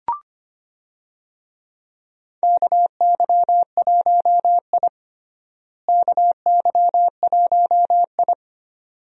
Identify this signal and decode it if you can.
Morse code, heard as a keyed tone, about 25 wpm, 700 Hz. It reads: KY1I KY1I
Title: USA Call Signs - 4 Characters - Repeated 2x - 25wpm